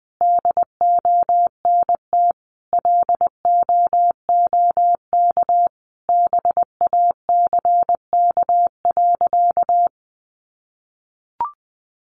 Morse code, heard as a keyed tone, about 20 wpm, 700 Hz. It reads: DONT LOOK BACK.
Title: Sentences from Top 300 Words 20wpm